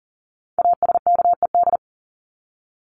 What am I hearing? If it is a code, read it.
Morse code, heard as a keyed tone, about 40 wpm, 700 Hz. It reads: ASKED
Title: ED Suffix - Top 100 Words with Base Words 40wpm